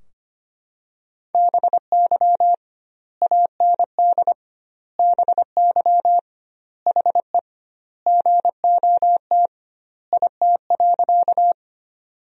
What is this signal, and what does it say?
Morse code, heard as a keyed tone, about 25 wpm, 700 Hz. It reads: BY AND BY HE GOT IT.
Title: 2x Word-Spacing - Sentences from Top 300 Words - 25wpm